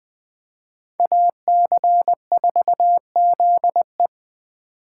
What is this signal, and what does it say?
Morse code, heard as a keyed tone, about 20 wpm, 700 Hz. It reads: AC4ZE
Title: Speed Racing - USA Call Signs 17wpm